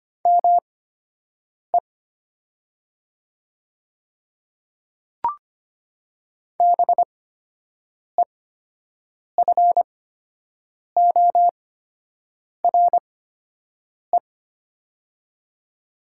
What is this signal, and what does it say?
Morse code, heard as a keyed tone, about 25 wpm, spaced as for 7 wpm, 690 Hz. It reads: ME BEFORE
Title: 8x Character Spacing - Top 500 Words - 25wpm